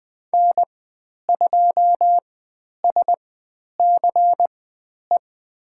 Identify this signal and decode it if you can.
Morse code, heard as a keyed tone, about 20 wpm, spaced as for 10 wpm, 700 Hz. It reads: N2SCE
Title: 095 Lesson 16g - Letter c - Call Signs 20wpm